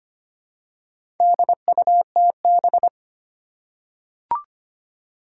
Morse code, heard as a keyed tone, about 25 wpm, 700 Hz. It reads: DUTB
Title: Four Challenging Letter-Number Combinations 25wpm